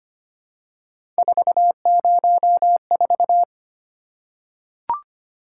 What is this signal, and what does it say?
Morse code, heard as a keyed tone, about 25 wpm, 700 Hz. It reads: 404
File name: Three Letter-Number Combinations - 25wpm